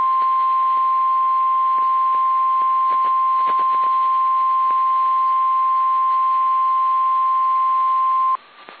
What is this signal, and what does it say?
Emergency Alert System (SAME) tones (the 1050 Hz attention tone). It is heard from 0.8 to 8.4 s.
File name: eas / KSDZKDJL / KSDZKDJL